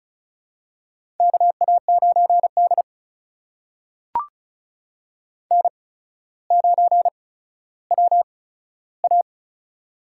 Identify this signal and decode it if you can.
Morse code, heard as a keyed tone, about 36 wpm, spaced as for 15 wpm, 700 Hz. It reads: KA9D N9WA
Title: USA Call Signs - 4 Characters - 8x Character Spacing - 35wpm